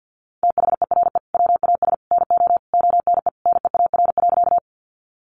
Morse code, heard as a keyed tone, about 50 wpm, 700 Hz. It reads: THERE WAS NO ONE NEAR.
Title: Sentences from Top 200 Words 50wpm